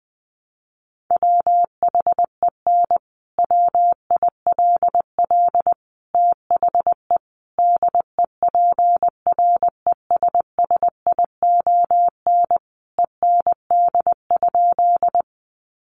Morse code, heard as a keyed tone, about 20 wpm, 700 Hz. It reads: WHEN WILL THE DEPRESSION END?
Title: Sentences from Top 1300 Words 20wpm